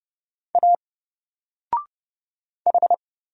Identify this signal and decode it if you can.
Morse code, heard as a keyed tone, about 31 wpm, 700 Hz. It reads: A H
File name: Single-Letter-Number-30wpm